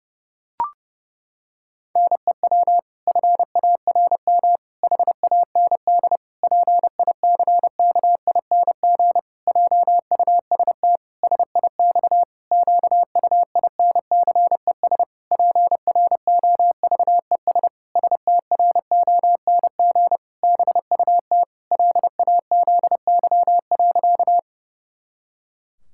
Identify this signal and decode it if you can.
Morse code, heard as a keyed tone, about 30 wpm, 700 Hz. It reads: NEW FARM HAND PICKING JUST SIX QUINCES PROVES STRONG BUT LAZY.
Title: Pangram Challenge v2 - 30wpm